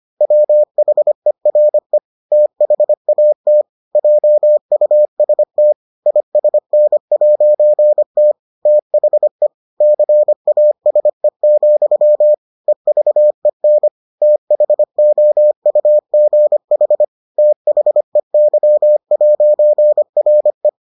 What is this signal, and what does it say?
Morse code, heard as a keyed tone, about 25 wpm, 600 Hz. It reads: WHERE THAT JUST ISN'T THE CASE, EVEN THOUGH THEY'RE